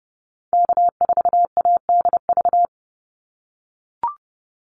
Morse code, heard as a keyed tone, about 30 wpm, 700 Hz. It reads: K4ADV
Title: NA SOTA Activator Callsigns 30wpm